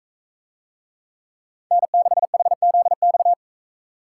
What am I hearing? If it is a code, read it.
Morse code, heard as a keyed tone, about 40 wpm, 700 Hz. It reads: N6HZX